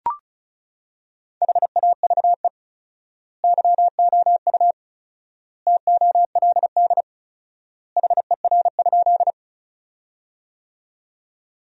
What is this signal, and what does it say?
Morse code, heard as a keyed tone, about 34 wpm, 700 Hz. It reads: HAVE YOU TOLD HER?
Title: 4x Word-Spacing - Sentences from Top 400 Words - 35wpm